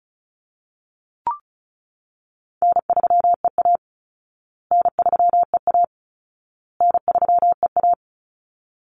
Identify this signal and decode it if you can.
Morse code, heard as a keyed tone, about 34 wpm, 700 Hz. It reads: N3EA N3EA N3EA
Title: Easy Call Signs Repeated 3x - 35wpm